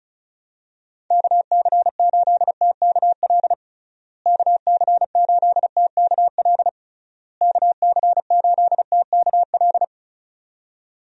Morse code, heard as a keyed tone, about 34 wpm, 700 Hz. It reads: KC8TKL KC8TKL KC8TKL
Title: USA Call Signs - 6 Characters - Repeated 3x - 35wpm